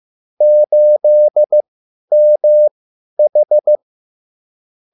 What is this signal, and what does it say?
Morse code, heard as a keyed tone, about 15 wpm, spaced as for 10 wpm, 600 Hz. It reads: 8MH